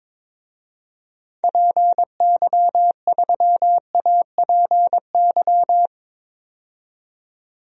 Morse code, heard as a keyed tone, about 22 wpm, 700 Hz. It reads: PY3APY
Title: DXCC-100 Call Signs - 6 Characters - 22wpm